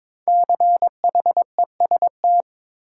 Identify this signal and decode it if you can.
Morse code, heard as a keyed tone, about 22 wpm, 700 Hz. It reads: CHEST
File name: 5 Letter Words - First 3 Letters Given - 22wpm